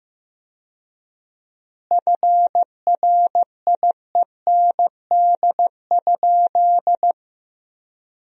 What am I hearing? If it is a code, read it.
Morse code, heard as a keyed tone, about 15 wpm, 700 Hz. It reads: FRIEND?
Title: 4x Word-Spacing - Sentences from Top 300 Words - 15wpm